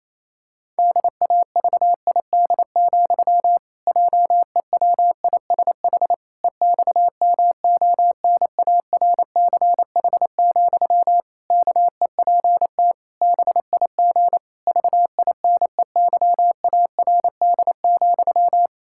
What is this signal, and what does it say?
Morse code, heard as a keyed tone, about 28 wpm, 700 Hz. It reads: DAVID, JEWISH EXMONARCH, KEPT BIG VINEYARD,
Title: Pangram Challenge v2 - 28wpm